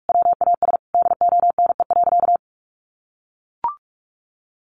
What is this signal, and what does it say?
Morse code, heard as a keyed tone, about 44 wpm, 700 Hz. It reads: WAS DONE.
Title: Sentences from Top 1400 Words 45wpm